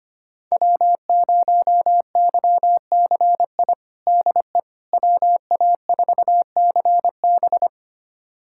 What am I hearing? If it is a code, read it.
Morse code, heard as a keyed tone, about 25 wpm, 700 Hz. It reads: W0YCI DE WA4CB
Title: USA Callsign DE Callsign - 5 Characters - 25wpm